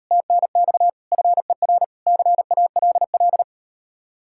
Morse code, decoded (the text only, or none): TNX FER CALL